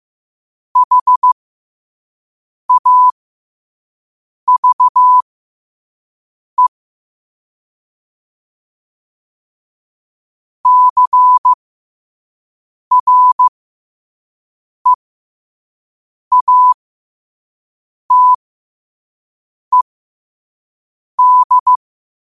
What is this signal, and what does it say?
Morse code, heard as a keyed tone, about 15 wpm, spaced as for 5 wpm, 1000 Hz. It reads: HAVE CREATED
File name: 5wpm